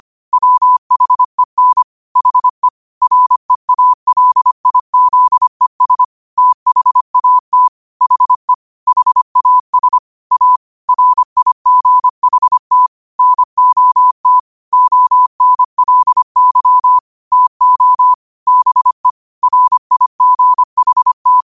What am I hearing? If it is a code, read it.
Morse code, heard as a keyed tone, about 25 wpm, 1000 Hz. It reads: WHEN HE REALIZES THAT HE HAS A RIGHT NOT ONLY TO BE RIGHT
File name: qotdfiles / 25wpm